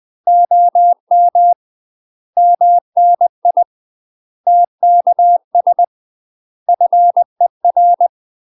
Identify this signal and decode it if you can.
Morse code, heard as a keyed tone, about 20 wpm, 700 Hz. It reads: OM MNI TKS FER
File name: UK MASTER QSOs TRAINING / 20WPM / UK MASTER QSOs 0014